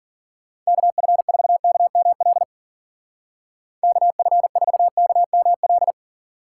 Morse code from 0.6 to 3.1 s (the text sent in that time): KF4KML